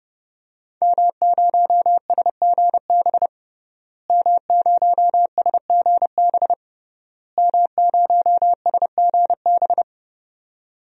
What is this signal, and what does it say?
Morse code, heard as a keyed tone, about 30 wpm, 700 Hz. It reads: M0SGB M0SGB M0SGB
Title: DXCC-100 Call Signs - 5 Characters - Repeated 3x - 30wpm